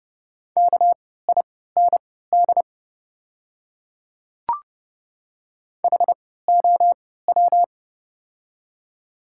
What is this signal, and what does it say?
Morse code, heard as a keyed tone, about 30 wpm, spaced as for 17 wpm, 700 Hz. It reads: KIND HOW
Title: 3x Character Spacing - Top 200 Words - 30wpm